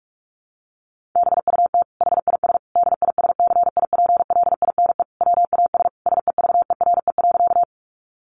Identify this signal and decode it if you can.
Morse code, heard as a keyed tone, about 44 wpm, 700 Hz. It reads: BUT HIS DISCIPLINE WAS SEVERE.
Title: Sentences from Top 1400 Words 45wpm